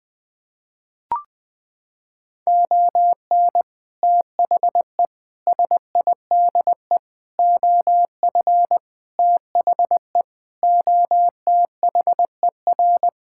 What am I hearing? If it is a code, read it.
Morse code, heard as a keyed tone, about 21 wpm, 700 Hz. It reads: ON THE SIDE OF THE OTHER
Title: Sentences with 3-Word Phrases - 20wpm